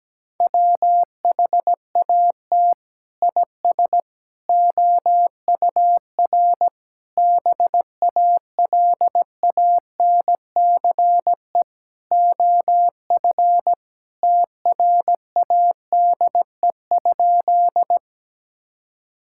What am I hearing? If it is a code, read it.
Morse code, heard as a keyed tone, about 17 wpm, 700 Hz. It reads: WHAT IS OUR BALANCE OF TRADE?
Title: Sentences from Top 1000 Words - Review Part 6 - 17wpm